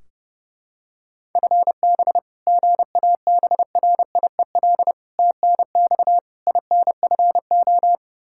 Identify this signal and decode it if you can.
Morse code, heard as a keyed tone, about 32 wpm, 700 Hz. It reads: FB GABRIEL TNX INFO
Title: Practice QSO - Exchange 3 - More Info - 30wpm